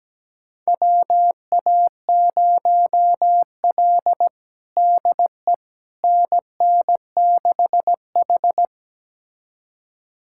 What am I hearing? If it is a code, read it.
Morse code, heard as a keyed tone, about 17 wpm, 700 Hz. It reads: WA0L DE NN6H
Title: USA Callsign DE Callsign - 4 Characters - 17wpm